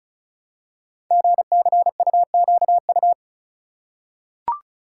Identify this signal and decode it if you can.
Morse code, heard as a keyed tone, about 36 wpm, 700 Hz. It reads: GCUQU
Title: Code Groups - 5 Letters - 35wpm